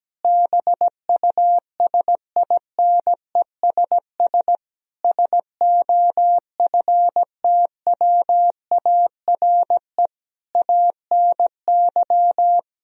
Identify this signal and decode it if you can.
Morse code, heard as a keyed tone, about 17 wpm, 700 Hz. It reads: BUSINESS SOFTWARE ANY
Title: Sentences from Top 1000 Words - Review Part 3 - 17wpm